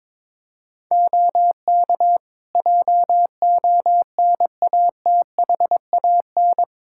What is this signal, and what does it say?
Morse code, heard as a keyed tone, about 22 wpm, 700 Hz. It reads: OK JONATHAN
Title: Practice QSO - Exchange 2 - Info - 22wpm